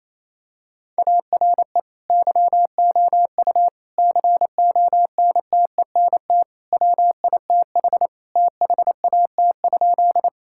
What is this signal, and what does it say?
Morse code, heard as a keyed tone, about 28 wpm, 700 Hz. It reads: ARE YOU CONTENT WITH THAT?
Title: Sentences from Top 1000 Words - Review Part 6 - 28wpm